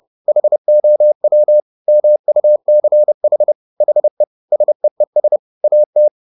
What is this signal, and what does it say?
Morse code, heard as a keyed tone, about 30 wpm, 600 Hz. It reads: HOW MUCH HE SEES AT
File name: the old man and the sea / the old man and the sea 1 30wpm